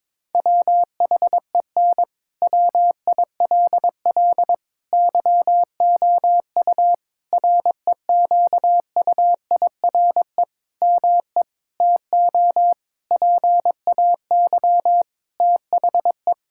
Morse code, heard as a keyed tone, about 21 wpm, 700 Hz. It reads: WHEN WILL YOU REQUIRE ME TO PAY THE
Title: Sentences from Top 1000 Words - Review Part 6 - 22wpm